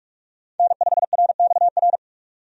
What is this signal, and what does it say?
Morse code, heard as a keyed tone, about 46 wpm, 700 Hz. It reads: N5RXR
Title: USA Call Signs - 5 Characters - 45wpm